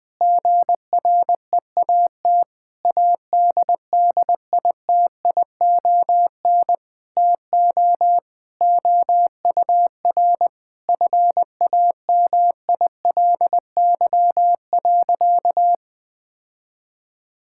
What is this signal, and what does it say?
Morse code, heard as a keyed tone, about 20 wpm, 700 Hz. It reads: GREAT ADDITION TO OUR FAMILY.
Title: Sentences from Top 1000 Words - Review Part 6 - 20wpm